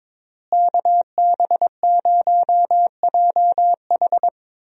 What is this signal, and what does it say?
Morse code, heard as a keyed tone, about 22 wpm, 700 Hz. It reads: KB0JH
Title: USA Call Signs - 5 Characters - Repeated 3x - 22wpm